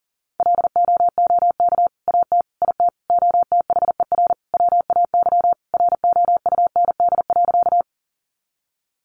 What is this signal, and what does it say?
Morse code, heard as a keyed tone, about 40 wpm, 700 Hz. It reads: LOOK AT IT OTHER WAY ROUND.
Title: Sentences from Top 200 Words 40wpm